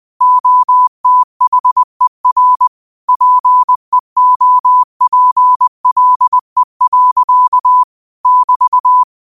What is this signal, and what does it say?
Morse code, heard as a keyed tone, about 20 wpm, 1000 Hz. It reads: OTHER PEOPLE. <BT>